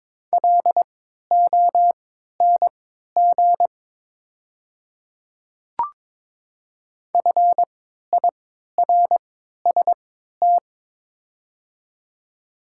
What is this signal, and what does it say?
Morse code, heard as a keyed tone, about 22 wpm, spaced as for 12 wpm, 700 Hz. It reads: LONG FIRST
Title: 3x Character Spacing - Top 100 Words - 22wpm